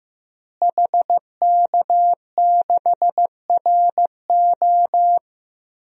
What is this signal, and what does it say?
Morse code, heard as a keyed tone, about 15 wpm, 700 Hz. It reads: HK6RO
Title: DXCC-100 Call Signs - 5 Characters - Repeated 3x - 15wpm